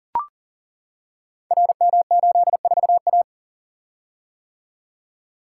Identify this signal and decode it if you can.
Morse code, heard as a keyed tone, about 38 wpm, 700 Hz. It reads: RM84A
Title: Code Groups - 5 Letter:Numbers - 40wpm